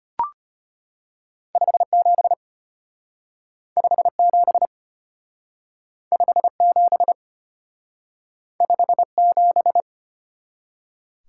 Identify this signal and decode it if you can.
Morse code, heard as a keyed tone, about 32 wpm, 700 Hz. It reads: 57 57 57 57